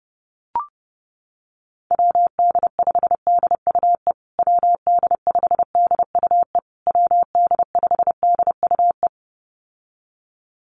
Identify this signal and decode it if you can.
Morse code, heard as a keyed tone, about 30 wpm, 700 Hz. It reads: WD5DUE WD5DUE WD5DUE
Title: USA Call Signs Repeated 3x 30wpm